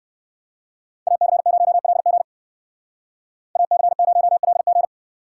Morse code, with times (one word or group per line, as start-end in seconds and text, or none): 1.1-2.2 AC0LG
3.6-4.9 AC0LG